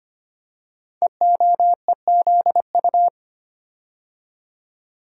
Morse code, heard as a keyed tone, about 25 wpm, 700 Hz. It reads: EOEZU
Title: Code Groups - 5 Letters - 25wpm